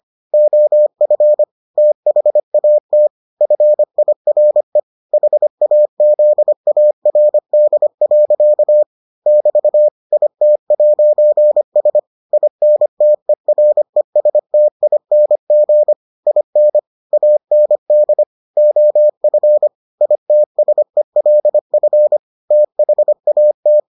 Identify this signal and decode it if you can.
Morse code, heard as a keyed tone, about 25 wpm, 600 Hz. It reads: OF THAT FIRE HAZARD. <BT> IT'S INTERESTING IN AND OF ITSELF THAT